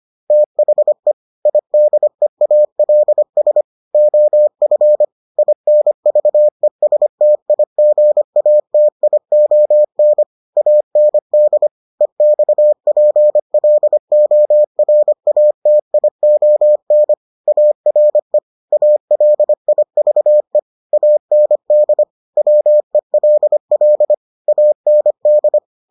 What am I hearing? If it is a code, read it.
Morse code, heard as a keyed tone, about 25 wpm, 600 Hz. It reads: THE IDEALS OF INVESTIGATION AND EXPLORATION ARE ALIVE AND WELL AND